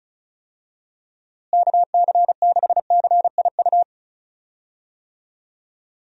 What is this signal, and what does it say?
Morse code, heard as a keyed tone, about 34 wpm, 700 Hz. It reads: KC6CIU
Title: USA Call Signs - 6 Characters - Repeated 3x - 35wpm